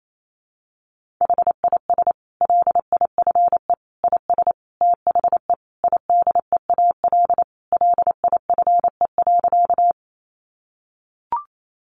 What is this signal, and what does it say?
Morse code, heard as a keyed tone, about 28 wpm, 700 Hz. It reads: HIS LIFE IS THE IDEAL LIFE.
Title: Sentences from Top 1400 Words 28wpm